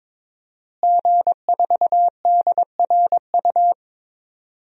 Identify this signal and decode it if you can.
Morse code, heard as a keyed tone, about 22 wpm, 700 Hz. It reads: G4DRU
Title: DXCC-100 Call Signs - 5 Characters - Repeated 3x - 22wpm